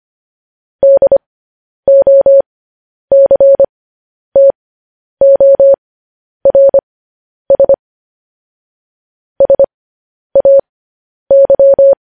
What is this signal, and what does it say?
Morse code, heard as a keyed tone, about 25 wpm, spaced as for 10 wpm, 550 Hz. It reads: DOCTORS SAY
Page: news headlines in morse code at 10 wpm